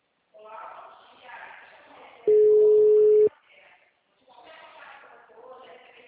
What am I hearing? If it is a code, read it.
Ringback tone (Europe, 425 Hz), with speech.